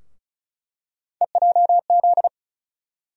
Morse code, heard as a keyed tone, about 34 wpm, 700 Hz. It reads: EJZ
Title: USA Call Sign Postfixes - 3 Letters 35wpm